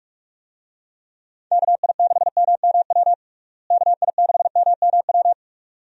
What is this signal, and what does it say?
Morse code, heard as a keyed tone, about 46 wpm, 700 Hz. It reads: KI6MMW KI6MMW
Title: USA Call Signs - 6 Characters - Repeated 2x - 45wpm